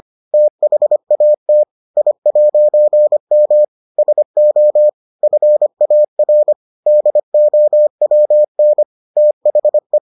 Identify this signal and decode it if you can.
Morse code, heard as a keyed tone, about 25 wpm, 600 Hz. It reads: THAT I'M SO FAR DOWN THE